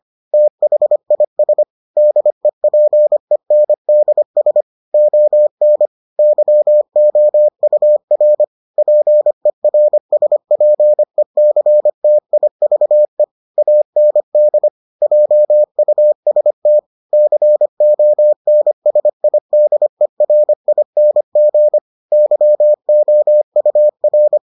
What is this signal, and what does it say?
Morse code, heard as a keyed tone, about 25 wpm, 600 Hz. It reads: THIS DEPENDS ON YOUR PERSPECTIVE AND JUST CONSIDERING YOUR